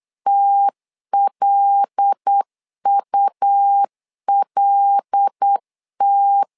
Morse code, heard as a keyed tone, about 8 wpm, 800 Hz. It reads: TLULT
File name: LEZIONI / 5 LBVC / 5 L 8wpm 2831